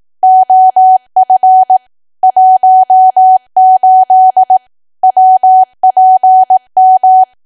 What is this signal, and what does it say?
Morse code, heard as a keyed tone, about 18 wpm, 750 Hz. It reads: OF 18 WPM
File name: cwa / W1AW-Level2 / 18WPM